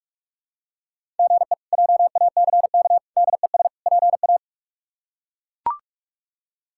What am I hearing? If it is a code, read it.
Morse code, heard as a keyed tone, about 41 wpm, 700 Hz. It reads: GE JACK DES PA
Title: K1USN SST QSO Practice - 45wpm